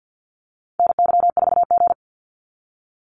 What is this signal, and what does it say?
Morse code, heard as a keyed tone, about 50 wpm, 700 Hz. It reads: NY4G
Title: NA SOTA Activator Callsigns 50wpm